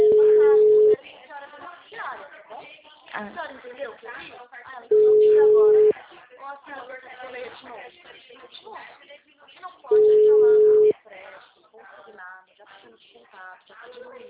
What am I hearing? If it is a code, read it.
Ringback tone (Europe, 425 Hz), with speech.
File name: out- -2027- - - .31300